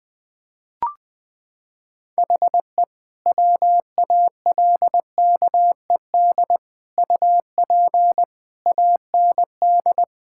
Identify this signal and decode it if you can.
Morse code, heard as a keyed tone, about 20 wpm, 700 Hz. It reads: HE WALKED UP AND
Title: Sentences with 4-Word Phrases - 20wpm